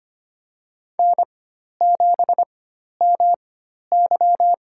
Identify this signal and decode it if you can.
Morse code, heard as a keyed tone, about 25 wpm, spaced as for 12 wpm, 700 Hz. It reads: N7MY
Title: USA Call Signs - 4 Characters - 4x Character Spacing - 25wpm